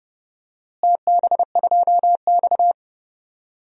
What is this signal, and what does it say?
Morse code, heard as a keyed tone, about 32 wpm, 700 Hz. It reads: TB2X
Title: DXCC-100 Call Signs - 4 Characters - Repeated 2x - 30wpm